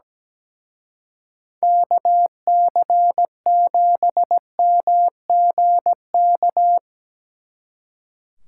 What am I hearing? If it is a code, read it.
Morse code, heard as a keyed tone, about 17 wpm, 700 Hz. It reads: KC7MGK
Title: Speed Racing - USA 2x3 Call Signs 17wpm